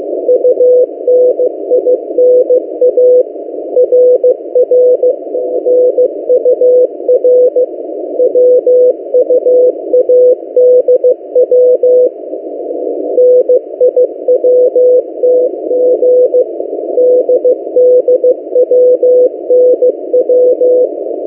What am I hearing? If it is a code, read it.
Morse code, heard as a keyed tone, about 15 wpm, 500 Hz. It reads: UNINA RRNUR WUADW NIWTG DDWNW
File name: M08a- -1355utc-8096khz